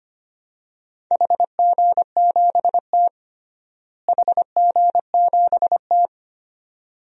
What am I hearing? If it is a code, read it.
Morse code, heard as a keyed tone, about 25 wpm, 700 Hz. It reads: HG7T HG7T
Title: DXCC-100 Call Signs - 4 Characters - Repeated 2x - 25wpm